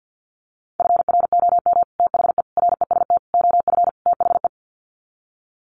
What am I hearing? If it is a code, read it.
Morse code, heard as a keyed tone, about 50 wpm, 700 Hz. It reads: FROM THE REST OF THE
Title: Sets of 5 Words 50wpm